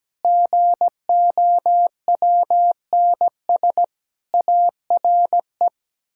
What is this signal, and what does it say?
Morse code, heard as a keyed tone, about 17 wpm, 700 Hz. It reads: GOWNS ARE
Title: Pangram Challenge v2 - 17wpm